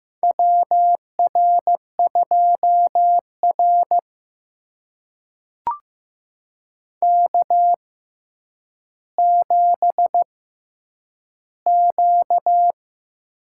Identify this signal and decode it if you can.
Morse code, heard as a keyed tone, about 15 wpm, spaced as for 9 wpm, 700 Hz. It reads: WR2R K7Q
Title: USA Call Signs - 4 Characters - 6x Character Spacing - 15wpm